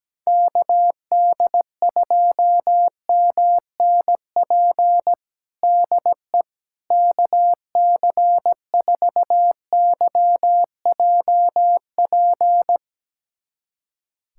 Morse code, heard as a keyed tone, about 17 wpm, 700 Hz. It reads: KD2MNP DE KC4YJP
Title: USA Callsign DE Callsign 17wpm